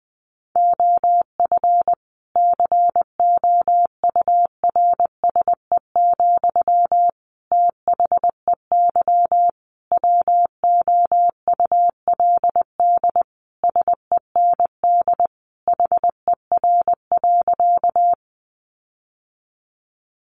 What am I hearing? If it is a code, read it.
Morse code, heard as a keyed tone, about 20 wpm, 700 Hz. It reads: OF COURSE, THEY WOULD SEND HER.
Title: Sentences from Top 1500 Words 20wpm